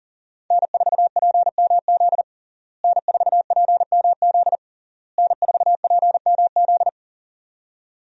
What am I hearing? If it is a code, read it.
Morse code, heard as a keyed tone, about 40 wpm, 700 Hz. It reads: N4PMZ N4PMZ N4PMZ
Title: USA Call Signs - 5 Characters - Repeated 3x - 40wpm